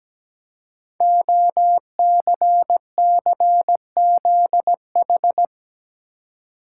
Morse code, heard as a keyed tone, about 17 wpm, 700 Hz. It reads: OCCZH
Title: Code Groups - 5 Letters - 17wpm